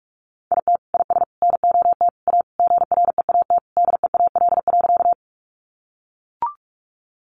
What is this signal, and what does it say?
Morse code, heard as a keyed tone, about 44 wpm, 700 Hz. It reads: IT IS NOT A GREAT DEAL.
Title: Sentences from Top 800 Words 45wpm